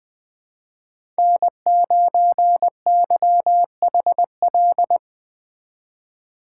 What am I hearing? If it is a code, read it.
Morse code, heard as a keyed tone, about 21 wpm, 700 Hz. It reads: N9YHL